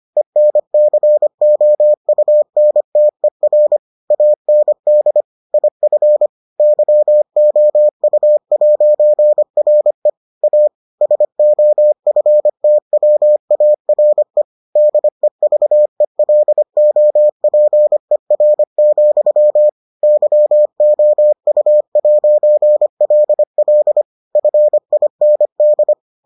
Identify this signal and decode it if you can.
Morse code, heard as a keyed tone, about 25 wpm, 600 Hz. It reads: ENCOUNTER AND IF YOU'RE A SOFTWARE DEVELOPER, YOU'LL FIND